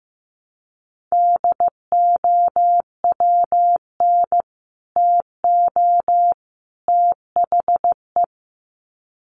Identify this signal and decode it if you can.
Morse code, heard as a keyed tone, about 15 wpm, 700 Hz. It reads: DOWN TO THE
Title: Sets of 3 Words 15wpm